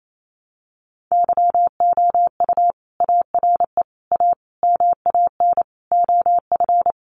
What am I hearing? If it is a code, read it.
Morse code, heard as a keyed tone, about 28 wpm, 700 Hz. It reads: YOU ARE A MAN OF
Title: Sentences from Top 1100 Words 28wpm